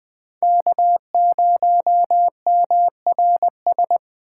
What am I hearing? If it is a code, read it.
Morse code, heard as a keyed tone, about 23 wpm, 700 Hz. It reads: K0MRS
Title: Speed Racing - USA Call Signs 20wpm